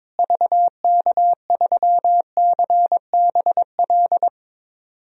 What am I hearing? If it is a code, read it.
Morse code, heard as a keyed tone, about 22 wpm, 700 Hz. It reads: VK3CBL
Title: DXCC-100 Call Signs - 6 Characters - Repeated 2x - 22wpm